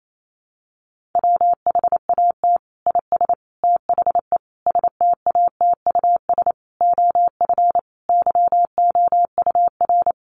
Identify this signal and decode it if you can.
Morse code, heard as a keyed tone, about 28 wpm, 700 Hz. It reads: WHAT IS THE STATUS OF YOUR
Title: Sentences from Top 1400 Words 28wpm